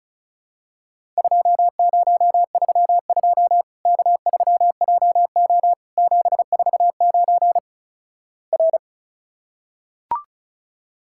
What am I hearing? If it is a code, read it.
Morse code, heard as a keyed tone, about 36 wpm, 700 Hz. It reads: 2032 K3JO 749 R
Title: Worked All Europe DX Contest - QTC Exchange - 35wpm